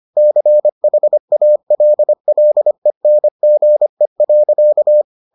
Morse code, heard as a keyed tone, about 25 wpm, 600 Hz. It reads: CHALLENGE.